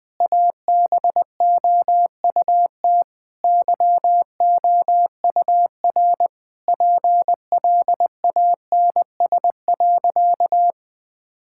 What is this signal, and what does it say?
Morse code, heard as a keyed tone, about 20 wpm, 700 Hz. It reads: ABOUT YOUR PLANS.
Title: Sentences with 5-Word Phrases - 20wpm